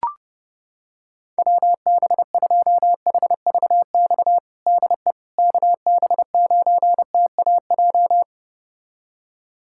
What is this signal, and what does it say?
Morse code, heard as a keyed tone, about 30 wpm, 700 Hz. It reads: WB2HVX DE KB9TAJ
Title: USA Callsign DE Callsign - 6 Characters - 30wpm